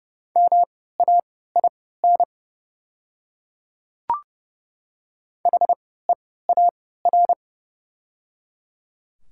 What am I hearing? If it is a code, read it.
Morse code, heard as a keyed tone, about 30 wpm, spaced as for 17 wpm, 690 Hz. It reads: MAIN HEAR
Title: 3x Character Spacing - Top 300 Words - 30wpm